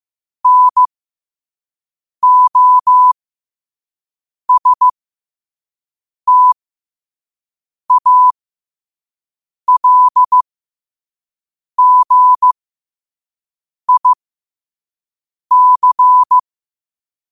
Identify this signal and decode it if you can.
Morse code, heard as a keyed tone, about 15 wpm, spaced as for 5 wpm, 1000 Hz. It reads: NOSTALGIC
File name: qotdfiles / 5wpm